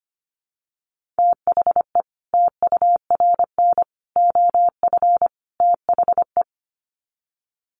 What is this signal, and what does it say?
Morse code, heard as a keyed tone, about 25 wpm, 700 Hz. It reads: THE TURN OF THE
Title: Sets of 4 Words 25wpm